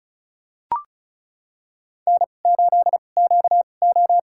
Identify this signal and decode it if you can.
Morse code, heard as a keyed tone, about 34 wpm, spaced as for 25 wpm, 700 Hz. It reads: N8QO
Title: USA Call Signs - 4 Characters - 2x Character Spacing - 35wpm